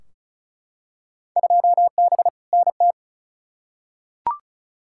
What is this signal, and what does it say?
Morse code, heard as a keyed tone, about 34 wpm, 700 Hz. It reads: 2B NT
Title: Field Day Exchanges Encore - 35wpm